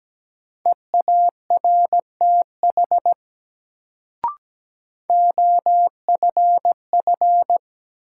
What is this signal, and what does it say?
Morse code, heard as a keyed tone, about 17 wpm, 700 Hz. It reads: EARTH OFF
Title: Sentences from Top 200 Words - New Word Drill - 17wpm